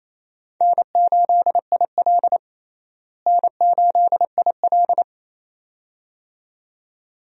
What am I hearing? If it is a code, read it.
Morse code, heard as a keyed tone, about 28 wpm, 700 Hz. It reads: N8IL N8IL
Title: USA Call Signs - 4 Characters - Repeated 2x - 28wpm